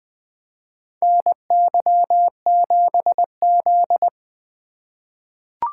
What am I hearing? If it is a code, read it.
Morse code, heard as a keyed tone, about 21 wpm, 700 Hz. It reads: NY7Z
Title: K1USN SST QSO Practice - 20wpm